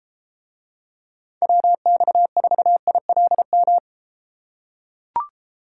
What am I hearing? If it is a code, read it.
Morse code, heard as a keyed tone, about 33 wpm, 700 Hz. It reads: WX4ILM